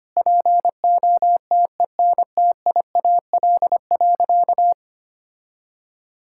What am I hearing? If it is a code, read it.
Morse code, heard as a keyed tone, about 25 wpm, 700 Hz. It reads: POTENTIAL.
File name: practice / 25wpm / Sentences with 9-Word Phrases - 25wpm